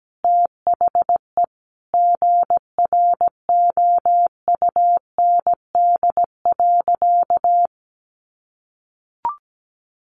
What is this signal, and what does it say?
Morse code, heard as a keyed tone, about 17 wpm, 700 Hz. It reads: THE GROUND.
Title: Sentences from Top 1500 Words 17wpm